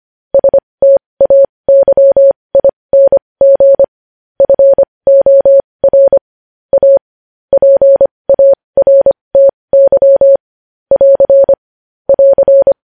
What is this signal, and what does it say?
Morse code, heard as a keyed tone, about 25 wpm, spaced as for 20 wpm, 550 Hz. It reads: STAYING FOR A PARTY <AR> <AR>
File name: podcast / News headlines-1x-25x20